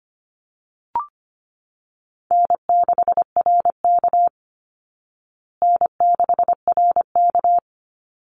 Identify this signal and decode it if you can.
Morse code, heard as a keyed tone, about 25 wpm, 700 Hz. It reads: N6RK N6RK
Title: Easy Call Signs Repeated 3x - 25wpm